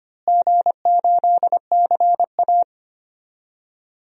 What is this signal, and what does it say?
Morse code, heard as a keyed tone, about 25 wpm, 700 Hz. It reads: G8CA
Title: DXCC-100 Call Signs - 4 Characters - Repeated 2x - 25wpm